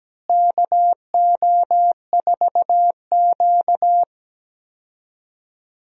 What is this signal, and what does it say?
Morse code, heard as a keyed tone, about 17 wpm, 700 Hz. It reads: KO4Q
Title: USA Call Signs - 4 Characters - Repeated 3x - 17wpm